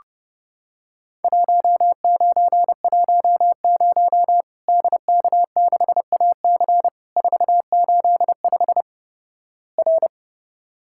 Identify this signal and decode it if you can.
Morse code, heard as a keyed tone, about 30 wpm, 700 Hz. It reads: 1910 DK6AC 485 R
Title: Worked All Europe DX Contest - QTC Exchange - 30wpm